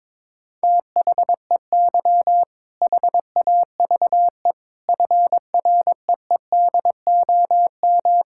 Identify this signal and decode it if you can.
Morse code, heard as a keyed tone, about 22 wpm, 700 Hz. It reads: THEY HAVE FREEDOM
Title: Sentences from Top 1000 Words - Review Part 5 - 22wpm